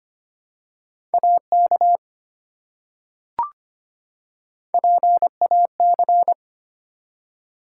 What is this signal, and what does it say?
Morse code, heard as a keyed tone, about 25 wpm, 700 Hz. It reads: AK PAC
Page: morse-code-ninja-practice-25wpm